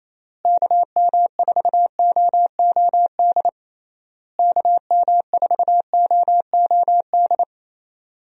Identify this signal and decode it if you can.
Morse code, heard as a keyed tone, about 28 wpm, 700 Hz. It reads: KM4OOD KM4OOD
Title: USA Call Signs - 6 Characters - Repeated 3x - 28wpm